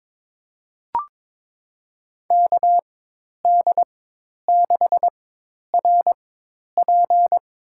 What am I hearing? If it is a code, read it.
Morse code, heard as a keyed tone, about 22 wpm, spaced as for 10 wpm, 700 Hz. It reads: KD6RP
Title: USA Call Signs - 4x Character Spacing - 22wpm